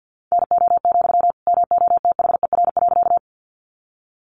Morse code, heard as a keyed tone, about 50 wpm, 700 Hz. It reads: NO, MOTHER.
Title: Sentences from Top 200 Words 50wpm